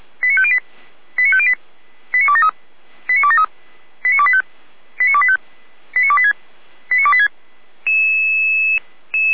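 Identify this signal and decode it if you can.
Five-tone selective call (ZVEI), heard as quick runs of five tones, tones of about 70 ms. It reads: 89508 89508 89273 89273 89285 89285 89286 89286
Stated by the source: ZVEI five-tone call 89280 received on SDR.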